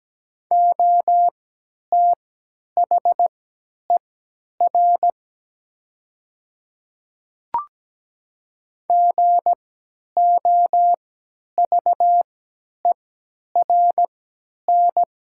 Morse code, heard as a keyed tone, about 17 wpm, spaced as for 10 wpm, 690 Hz. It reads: OTHER GOVERN
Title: 3x Character Spacing - Top 500 Words - 17wpm